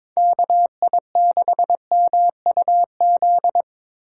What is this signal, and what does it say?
Morse code, heard as a keyed tone, about 26 wpm, 700 Hz. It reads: KI6MUZ
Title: Speed Racing - USA Call Signs 22wpm